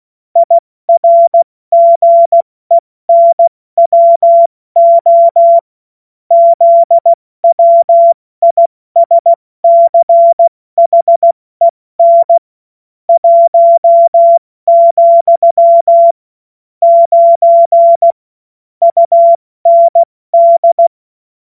Morse code, heard as a keyed tone, about 16 wpm, 680 Hz. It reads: IRGENWO ZWISCHEN 1, 9 UND
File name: Morsekurs / Witze / Witz 21. Dezember 2021, 70 F80